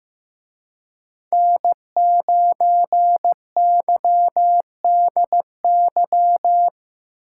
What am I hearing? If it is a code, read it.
Morse code, heard as a keyed tone, about 18 wpm, 700 Hz. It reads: N9YDY
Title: Speed Racing - USA 1x3 Call Signs 15wpm